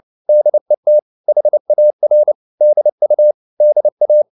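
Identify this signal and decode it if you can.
Morse code, heard as a keyed tone, about 29 wpm, 600 Hz. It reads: DET HAR DU DA
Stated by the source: Fast (QRQ) Morse at 29 wpm, from CW med Gnister.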